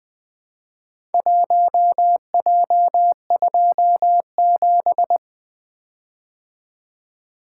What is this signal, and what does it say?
Morse code, heard as a keyed tone, about 20 wpm, 700 Hz. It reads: 1J27
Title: Four Letter-Number Combinations - 20wpm